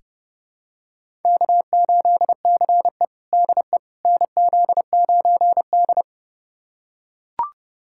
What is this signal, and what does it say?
Morse code, heard as a keyed tone, about 30 wpm, 700 Hz. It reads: K8CE DE NZ9D
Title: USA Callsign DE Callsign - 4 Characters - 30wpm